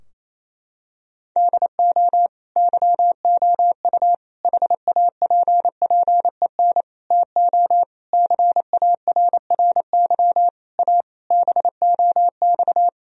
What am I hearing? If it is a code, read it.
Morse code, heard as a keyed tone, about 28 wpm, 700 Hz. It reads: DO YOU HAPPEN TO CARRY A BOX
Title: Sentences from Top 1000 Words - Review Part 2 - 28wpm